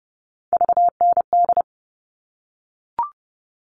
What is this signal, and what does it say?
Morse code, heard as a keyed tone, about 30 wpm, 700 Hz. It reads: VND
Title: All Cut Numbers - 3-Digit 30wpm